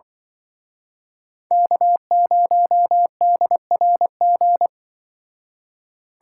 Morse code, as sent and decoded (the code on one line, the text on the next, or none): -.- ----- -.. .-. --.
K0DRG